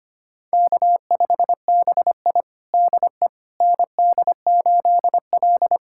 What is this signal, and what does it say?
Morse code, heard as a keyed tone, about 25 wpm, 700 Hz. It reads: K5BI DE ND8L
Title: USA Callsign DE Callsign - 4 Characters - 25wpm